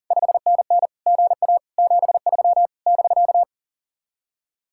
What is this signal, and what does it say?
Morse code, heard as a keyed tone, about 52 wpm, 700 Hz. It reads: 5NN GA 73 <BK>